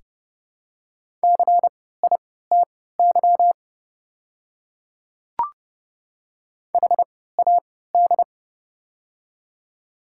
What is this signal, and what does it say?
Morse code, heard as a keyed tone, about 30 wpm, spaced as for 17 wpm, 690 Hz. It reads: CITY HAD
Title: 3x Character Spacing - Top 300 Words - 30wpm